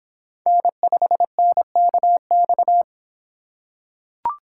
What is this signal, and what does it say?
Morse code, heard as a keyed tone, about 25 wpm, 700 Hz. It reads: N5NKX